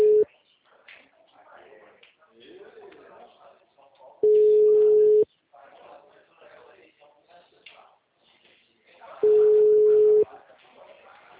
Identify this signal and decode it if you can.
Ringback tone (Europe, 425 Hz), with speech.